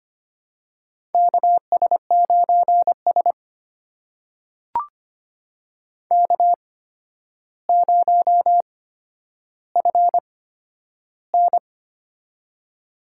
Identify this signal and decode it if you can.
Morse code, heard as a keyed tone, about 25 wpm, spaced as for 11 wpm, 700 Hz. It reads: KS9S K0FN
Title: USA Call Signs - 4 Characters - 8x Character Spacing - 25wpm